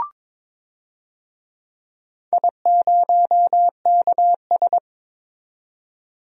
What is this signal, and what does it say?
Morse code, heard as a keyed tone, about 22 wpm, 700 Hz. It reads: I0KS
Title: DXCC-100 Call Signs - 4 Characters - 22wpm